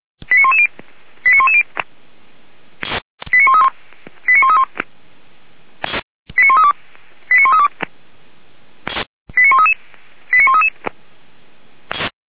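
Five-tone selective call (ZVEI), heard as quick runs of five tones, tones of about 70 ms. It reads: 89110 89110 89131 89131 89132 89132 89133 89133
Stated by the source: ZVEI five-tone selective call 89109 received by SDR.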